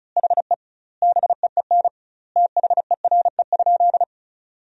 Morse code, heard as a keyed tone, about 36 wpm, 700 Hz. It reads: HE BEEN THERE?
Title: 2x Word-Spacing - Sentences from Top 100 Words - 35wpm